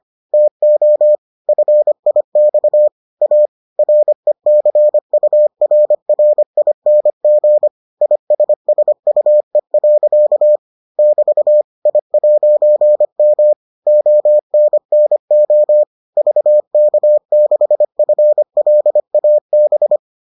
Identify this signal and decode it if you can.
Morse code, heard as a keyed tone, about 25 wpm, 600 Hz. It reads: TO FIX A RECURRING ISSUE. <BT> I'M ONNO VK6FLAB